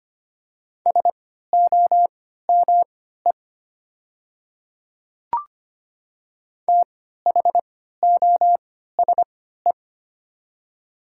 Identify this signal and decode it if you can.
Morse code, heard as a keyed tone, about 25 wpm, spaced as for 14 wpm, 700 Hz. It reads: SOME THOSE
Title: 3x Character Spacing - Top 500 Words - 25wpm